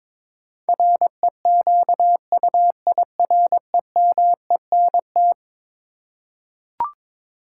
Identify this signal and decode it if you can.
Morse code, heard as a keyed tone, about 22 wpm, 700 Hz. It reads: REQUIREMENT
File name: practice / 22wpm / Eleven Letter Words - 22wpm